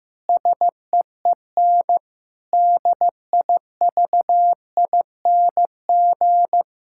Morse code, heard as a keyed tone, about 14 wpm, 700 Hz. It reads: SEEN DIVING
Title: Pangram Challenge v2 - 15wpm